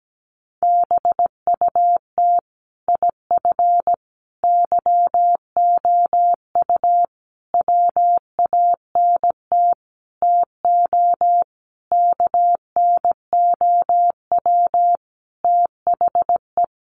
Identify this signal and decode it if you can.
Morse code, heard as a keyed tone, about 17 wpm, 700 Hz. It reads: BUT IF YOU WANT TO KNOW THE
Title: Sentences from Top 800 Words 17wpm